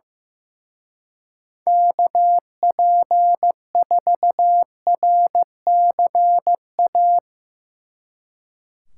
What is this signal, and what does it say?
Morse code, heard as a keyed tone, about 16 wpm, 700 Hz. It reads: KP4RCA
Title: Speed Racing - USA 2x3 Call Signs 15wpm